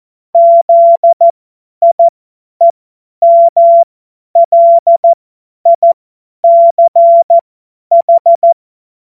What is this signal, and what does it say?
Morse code, heard as a keyed tone, about 14 wpm, spaced as for 10 wpm, 680 Hz. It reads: ZIEMLICH
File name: Morsekurs / Witze / Witz 18. Juli 2023 50 F70